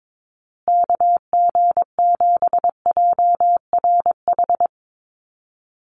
Morse code, heard as a keyed tone, about 22 wpm, 700 Hz. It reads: KG7JRH
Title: USA Call Signs Repeated 2x 22wpm